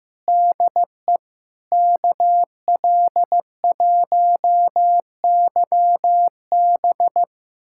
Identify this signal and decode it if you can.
Morse code, heard as a keyed tone, about 15 wpm, 700 Hz. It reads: DE KL1YB
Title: USA Callsign DE Callsign - 5 Characters - 15wpm